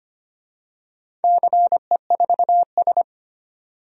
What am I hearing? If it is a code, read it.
Morse code, heard as a keyed tone, about 25 wpm, 690 Hz. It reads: CE4S
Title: DXCC-100 Call Signs - 4 Characters - 25wpm